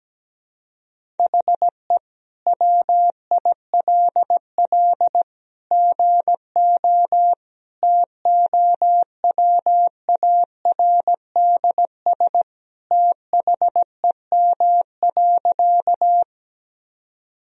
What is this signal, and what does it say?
Morse code, heard as a keyed tone, about 17 wpm, 700 Hz. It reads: HE WILL GO TOWARDS THEM.
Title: Sentences from Top 1000 Words - Review Part 2 - 17wpm